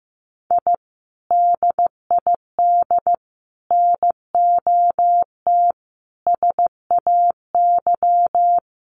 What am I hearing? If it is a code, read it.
Morse code, heard as a keyed tone, about 15 wpm, 700 Hz. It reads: I DID NOT SAY
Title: Sets of 5 Words Encore2 15wpm